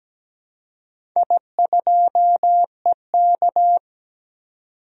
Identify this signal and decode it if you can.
Morse code, heard as a keyed tone, about 17 wpm, 700 Hz. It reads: I2EK
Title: DXCC-100 Call Signs - 4 Characters - 17wpm